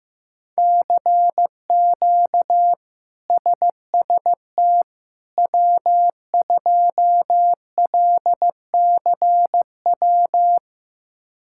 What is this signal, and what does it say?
Morse code, heard as a keyed tone, about 15 wpm, 700 Hz. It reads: CQ SST W2LCW
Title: K1USN SST QSO Practice - 15wpm